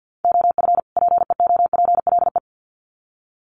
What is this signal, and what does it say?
Morse code, heard as a keyed tone, about 50 wpm, 700 Hz. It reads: OF PEOPLE
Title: Sets of 2 Words 50wpm